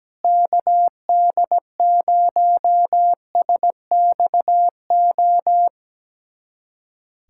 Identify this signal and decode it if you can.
Morse code, heard as a keyed tone, about 17 wpm, 700 Hz. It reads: KD0SXO
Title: USA Callsign DE Callsign 17wpm